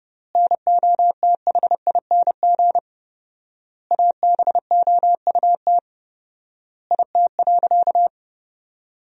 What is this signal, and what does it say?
Morse code, heard as a keyed tone, about 30 wpm, 700 Hz. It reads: NOTHING ABOUT IT.
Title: 4x Word-Spacing - Sentences from Top 500 Words - 30wpm